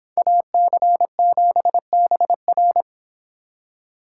Morse code, heard as a keyed tone, about 26 wpm, 700 Hz. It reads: AC7BR